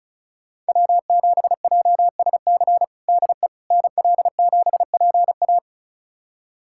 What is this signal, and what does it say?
Morse code, heard as a keyed tone, about 34 wpm, 700 Hz. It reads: W7JSC DE NL7PA
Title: USA Callsign DE Callsign - 5 Characters - 35wpm